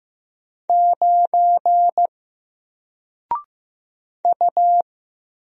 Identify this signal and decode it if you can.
Morse code, heard as a keyed tone, about 15 wpm, 700 Hz. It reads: 9 U
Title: Single Letter-Number Alt 15wpm